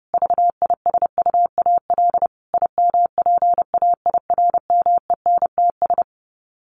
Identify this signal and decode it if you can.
Morse code, heard as a keyed tone, about 30 wpm, 700 Hz. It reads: VISUAL IMPAIRMENTS
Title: Sets of 5 Words Encore 30wpm